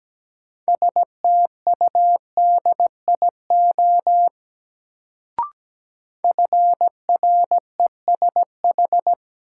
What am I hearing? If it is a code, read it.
Morse code, heard as a keyed tone, about 17 wpm, 700 Hz. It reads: STUDIO FRESH
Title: Sentences from Top 1200 Words - New Word Drill - 17wpm